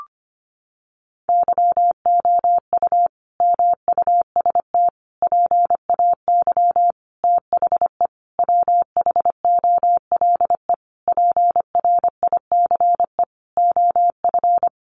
Morse code, heard as a keyed tone, about 25 wpm, 700 Hz. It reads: YOU MUST PAY THE WHOLE PRICE OF
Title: Sentences from Top 1400 Words 25wpm